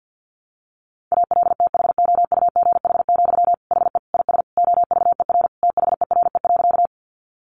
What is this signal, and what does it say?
Morse code, heard as a keyed tone, about 49 wpm, 700 Hz. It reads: ALTHOUGH, HE IS OVER THERE.
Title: Sentences from Top 600 Words 50wpm